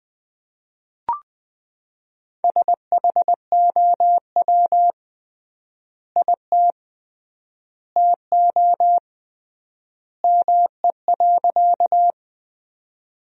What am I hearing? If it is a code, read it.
Morse code, heard as a keyed tone, about 20 wpm, 700 Hz. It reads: SHOW IT TO ME.
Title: 3x Word-Spacing - Sentences from Top 200 Words - 20wpm